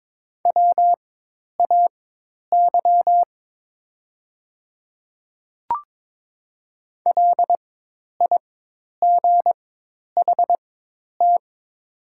Morse code, heard as a keyed tone, about 22 wpm, spaced as for 10 wpm, 700 Hz. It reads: WAY LIGHT
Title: 4x Character Spacing - Top 200 Words - 22wpm